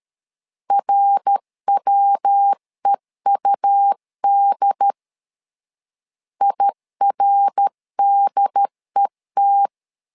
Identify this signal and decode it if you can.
Morse code, heard as a keyed tone, about 13 wpm, 800 Hz. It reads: RWEUD IRDET
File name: LEZIONI / 3 MWRO / 3 MWR 12wpm 1711